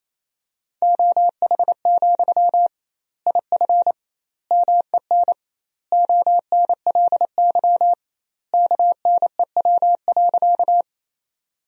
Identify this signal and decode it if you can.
Morse code, heard as a keyed tone, about 28 wpm, 700 Hz. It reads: OH, IF MEN ONLY KNEW.
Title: 2x Word-Spacing - Sentences from Top 500 Words - 28wpm